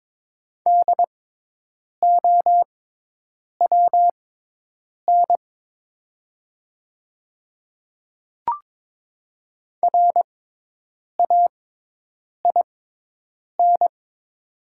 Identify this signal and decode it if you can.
Morse code, heard as a keyed tone, about 22 wpm, spaced as for 8 wpm, 700 Hz. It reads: DOWN RAIN
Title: 6x Character Spacing - Top 300 Words - 22wpm